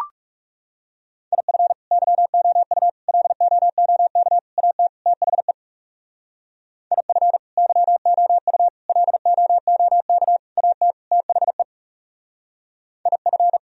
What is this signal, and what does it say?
Morse code, heard as a keyed tone, about 37 wpm, 700 Hz. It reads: IF YOU LOOK AT THE IF YOU LOOK AT THE IF
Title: Speed-Racing - Sets of 5 Words 30wpm